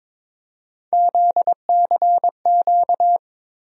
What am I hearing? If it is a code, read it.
Morse code, heard as a keyed tone, about 22 wpm, 700 Hz. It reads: ZCQ
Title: USA Call Sign Postfixes - 3 Letters 22wpm